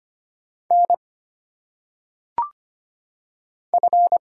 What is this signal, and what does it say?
Morse code, heard as a keyed tone, about 25 wpm, 700 Hz. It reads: N F
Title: Single Letters - Phonetic Pronunciation 25wpm